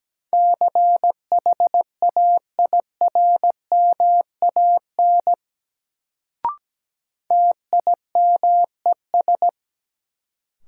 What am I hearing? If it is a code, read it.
Morse code, heard as a keyed tone, about 17 wpm, 700 Hz. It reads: CHAIRMAN TIMES
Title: Sets of 3 Words - New Word Drill - 17wpm